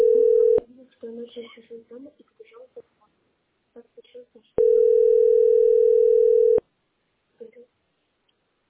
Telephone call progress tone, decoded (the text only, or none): ringback tone